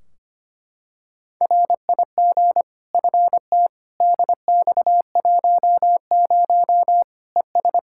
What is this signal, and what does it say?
Morse code, heard as a keyed tone, about 25 wpm, 700 Hz. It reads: RIG FT DX10 ES
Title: Practice QSO - Exchange 2 - Info - 25wpm